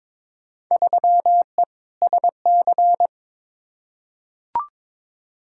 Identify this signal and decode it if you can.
Morse code, heard as a keyed tone, about 22 wpm, 700 Hz. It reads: 3E SC
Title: Field Day Exchanges Encore - 22wpm